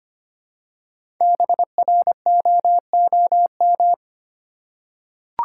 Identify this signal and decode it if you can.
Morse code, heard as a keyed tone, about 25 wpm, 700 Hz. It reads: BROOM
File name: practice / 25wpm / Easy Words with Hints - 25wpm